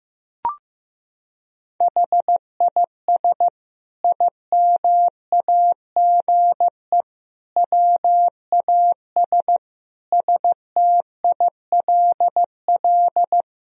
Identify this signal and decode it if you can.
Morse code, heard as a keyed tone, about 16 wpm, 700 Hz. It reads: HIS IMAGE WAS STILL
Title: Sentences from Top 1000 Words - Review Part 4 - 15wpm